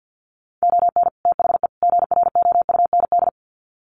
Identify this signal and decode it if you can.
Morse code, heard as a keyed tone, about 50 wpm, 700 Hz. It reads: ON THE GROUND
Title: Sets of 3 Words 50wpm